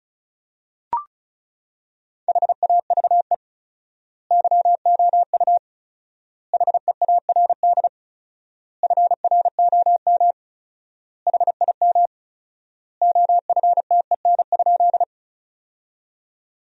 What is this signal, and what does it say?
Morse code, heard as a keyed tone, about 36 wpm, 700 Hz. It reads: HAVE YOU HEARD FROM HIM OFTEN?
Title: 4x Word-Spacing - Sentences from Top 400 Words - 35wpm